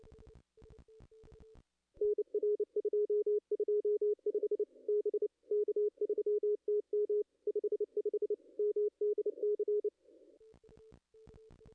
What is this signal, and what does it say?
Morse code, heard as a keyed tone, about 30 wpm, 440 Hz. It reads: NR 225 B K3TM 55 MDC
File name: contests / ss-cw-k5tr-op-n5rz -0200